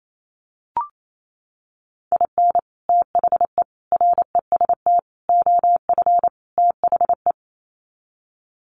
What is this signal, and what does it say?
Morse code, heard as a keyed tone, about 28 wpm, 700 Hz. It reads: IN THE REST OF THE
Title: Sets of 5 Words 28wpm